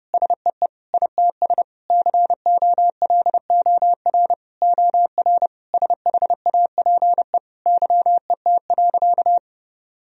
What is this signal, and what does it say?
Morse code, heard as a keyed tone, about 30 wpm, 700 Hz. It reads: SEE ITS COLOR OR SHAPE YET.
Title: Sentences from Top 1000 Words - Review Part 3 - 30wpm